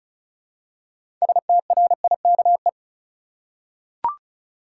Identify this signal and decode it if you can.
Morse code, heard as a keyed tone, about 34 wpm, 700 Hz. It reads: STRIKE
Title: Six Letter Words - 35wpm